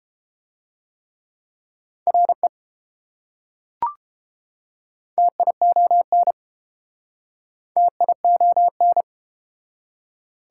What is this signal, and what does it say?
Morse code, heard as a keyed tone, about 30 wpm, 690 Hz. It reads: RE TION TION